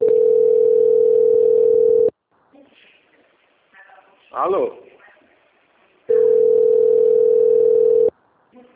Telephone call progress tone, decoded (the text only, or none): ringback tone